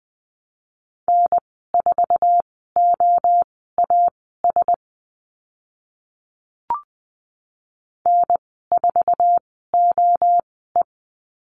Morse code, heard as a keyed tone, about 20 wpm, spaced as for 15 wpm, 700 Hz. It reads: N4OAS N4OE
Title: 048 Lesson 9g - Number 4 - Call Signs 20wpm